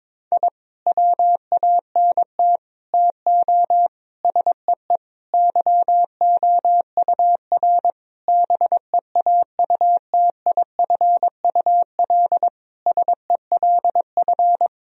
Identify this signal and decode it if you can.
Morse code, heard as a keyed tone, about 22 wpm, 700 Hz. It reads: I WANT TO SEE YOUR BEAUTIFUL SELF
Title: Sentences from Top 1000 Words - Review Part 1 - 22wpm